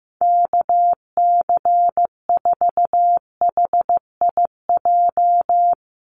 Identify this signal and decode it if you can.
Morse code, heard as a keyed tone, about 15 wpm, 700 Hz. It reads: KC4HIJ
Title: USA Call Signs Repeated 2x 15wpm